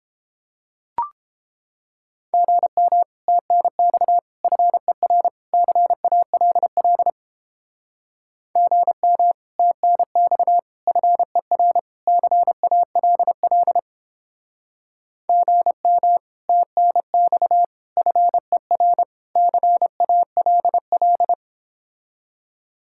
Morse code, heard as a keyed tone, about 29 wpm, 700 Hz. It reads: GM TNX FER CALL GM TNX FER CALL GM TNX FER CALL